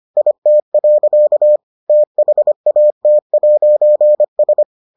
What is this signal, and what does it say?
Morse code, heard as a keyed tone, about 25 wpm, 600 Hz. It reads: IT. THAT'S